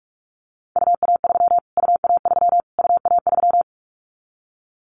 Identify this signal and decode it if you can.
Morse code, heard as a keyed tone, about 44 wpm, 700 Hz. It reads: UA3 UA3 UA3
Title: Callsign Prefixes for Top 25 DXCC - Repeated 3x - 45wpm